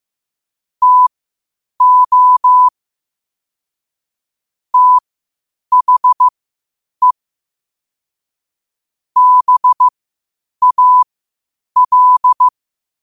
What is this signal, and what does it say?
Morse code, heard as a keyed tone, about 15 wpm, spaced as for 8 wpm, 1000 Hz. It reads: TO THE BAL
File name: qotdfiles / 8wpm